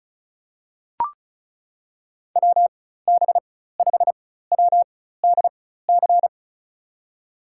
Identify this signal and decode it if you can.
Morse code, heard as a keyed tone, about 36 wpm, spaced as for 16 wpm, 700 Hz. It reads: WB5WDC
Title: USA Call Signs - 6 Characters - 4x Character Spacing - 35wpm